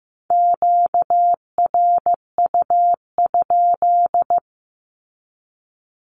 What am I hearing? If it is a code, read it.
Morse code, heard as a keyed tone, about 15 wpm, 700 Hz. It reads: QRU?
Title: Q-Codes with Spoken Meaning - 15wpm